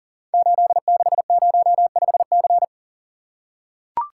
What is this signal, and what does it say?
Morse code, heard as a keyed tone, about 42 wpm, 700 Hz. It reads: 8605C
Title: Code Groups - 5 Letter:Numbers - 40wpm